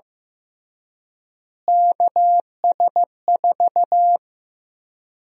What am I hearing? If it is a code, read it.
Morse code, heard as a keyed tone, about 15 wpm, 700 Hz. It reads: KS4
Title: USA Call Sign Prefixes - Repeated 2x - 15wpm